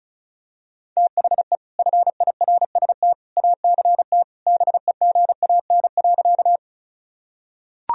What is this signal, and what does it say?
Morse code, heard as a keyed tone, about 34 wpm, 700 Hz. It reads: THE FIRST ACT BEGAN.
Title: Sentences from Top 1000 Words - Review Part 1 - 35wpm